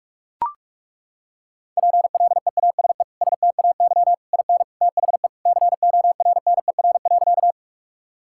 Morse code, heard as a keyed tone, about 41 wpm, 700 Hz. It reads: PLEASE STAY IN THE CORNER.
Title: Sentences from Top 1600 Words 45wpm